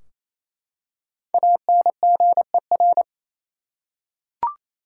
Morse code, heard as a keyed tone, about 28 wpm, 700 Hz. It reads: ANGER
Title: Five Letter Words - 28wpm